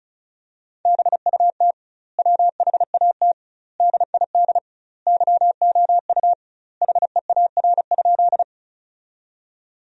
Morse code, heard as a keyed tone, about 34 wpm, 700 Hz. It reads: BUT WHAT DID YOU HEAR?
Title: 2x Word-Spacing - Sentences from Top 300 Words - 35wpm